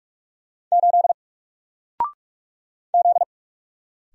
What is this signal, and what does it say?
Morse code, heard as a keyed tone, about 46 wpm, 700 Hz. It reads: 8 Z
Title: Single Character (Letter:Number:Punctuation:Prosign) Alt 45wpm